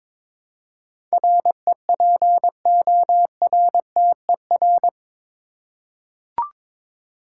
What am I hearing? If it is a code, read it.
Morse code, heard as a keyed tone, about 22 wpm, 700 Hz. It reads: REPORTER
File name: practice / 22wpm / Eight Letter Words - 22wpm